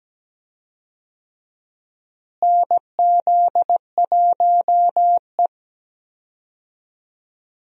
Morse code, heard as a keyed tone, about 17 wpm, 700 Hz. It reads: NZ1E
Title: USA Call Signs - 4 Characters - Repeated 2x - 17wpm